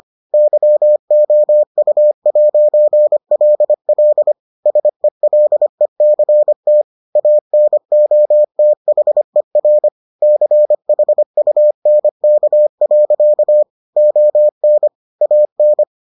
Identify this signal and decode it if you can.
Morse code, heard as a keyed tone, about 25 wpm, 600 Hz. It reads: YOU'LL SELECT ANOTHER CHUNK. ON AN